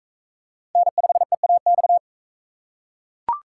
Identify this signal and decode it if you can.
Morse code, heard as a keyed tone, about 42 wpm, 700 Hz. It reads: N5EAX